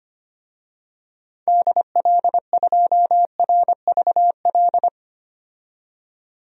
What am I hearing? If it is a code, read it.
Morse code, heard as a keyed tone, about 25 wpm, 700 Hz. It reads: DL2RVL
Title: DXCC-100 Call Signs - 6 Characters - 25wpm